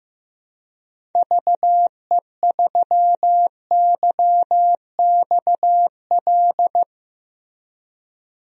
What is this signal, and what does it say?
Morse code, heard as a keyed tone, about 15 wpm, 700 Hz. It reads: VE3YXL
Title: DXCC-100 Call Signs - 6 Characters - Repeated 2x - 15wpm